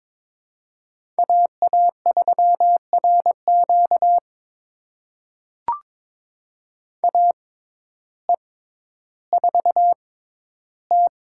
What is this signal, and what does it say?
Morse code, heard as a keyed tone, about 23 wpm, spaced as for 12 wpm, 700 Hz. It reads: AA3RQ AE4T
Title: USA Call Signs - 5 Characters - 6x Character Spacing - 22wpm